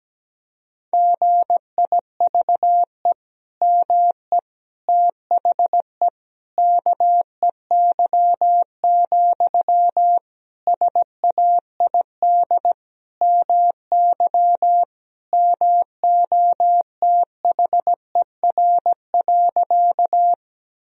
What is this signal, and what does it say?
Morse code, heard as a keyed tone, about 17 wpm, 700 Hz. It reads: GIVE ME THE KEY, SAID MY MOTHER.
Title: Sentences from Top 1000 Words - Review Part 4 - 17wpm